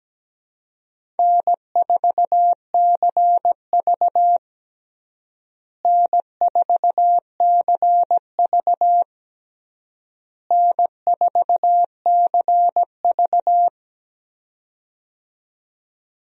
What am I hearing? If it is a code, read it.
Morse code, heard as a keyed tone, about 17 wpm, 700 Hz. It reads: N4CV N4CV N4CV
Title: USA Call Signs - 4 Characters - Repeated 3x - 17wpm